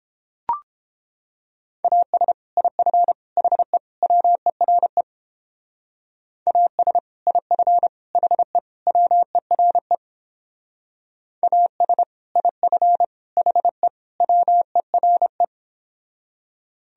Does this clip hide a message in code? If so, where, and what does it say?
Yes: Morse code from 1.8 to 15.4 s, heard as a keyed tone, about 29 wpm, 700 Hz. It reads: AS IF HE WERE AS IF HE WERE AS IF HE WERE